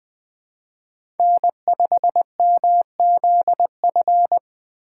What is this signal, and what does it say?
Morse code, heard as a keyed tone, about 20 wpm, 700 Hz. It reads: N5MZF
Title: USA Call Signs - 5 Characters - 2x Character Spacing - 20wpm